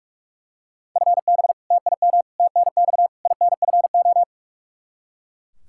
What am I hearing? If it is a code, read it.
Morse code, heard as a keyed tone, about 41 wpm, 700 Hz. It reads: FB TIM TNX INFO
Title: K1USN SST QSO Practice 45wpm